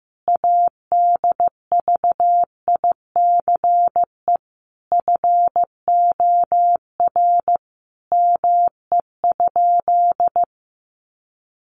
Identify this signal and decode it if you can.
Morse code, heard as a keyed tone, about 15 wpm, 700 Hz. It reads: ADVICE FOR ME?
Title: Sentences from Top 1100 Words 15wpm